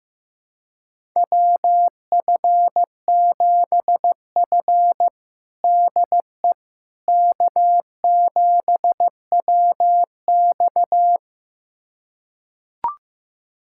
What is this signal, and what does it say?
Morse code, heard as a keyed tone, about 15 wpm, 700 Hz. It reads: WF7F DE K7WX
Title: USA Callsign DE Callsign - 4 Characters - 15wpm